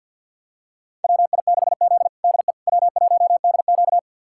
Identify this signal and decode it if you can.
Morse code, heard as a keyed tone, about 50 wpm, 700 Hz. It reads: WI6Z DE W1DQ
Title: USA Callsign DE Callsign - 4 Characters - 50wpm